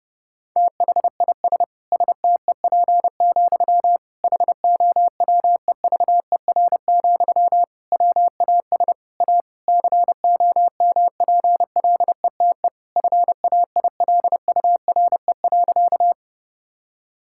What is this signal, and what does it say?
Morse code, heard as a keyed tone, about 30 wpm, 700 Hz. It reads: THIS STEP, HOWEVER, WAS A COMPLETE FAILURE.
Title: Sentences from Top 1000 Words - Review Part 6 - 30wpm